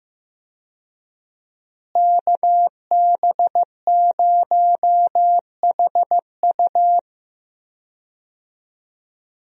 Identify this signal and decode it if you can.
Morse code, heard as a keyed tone, about 15 wpm, 700 Hz. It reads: KB0HU
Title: USA Call Signs - 5 Characters - Repeated 2x - 15wpm